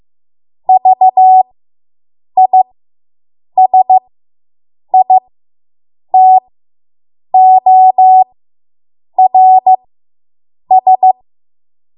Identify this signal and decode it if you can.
Morse code, heard as a keyed tone, about 15 wpm, spaced as for 7 wpm, 750 Hz. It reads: VISITORS